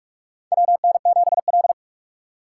Morse code, heard as a keyed tone, about 44 wpm, 700 Hz. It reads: WN7L
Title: USA Call Signs - 6x Character Spacing - 45wpm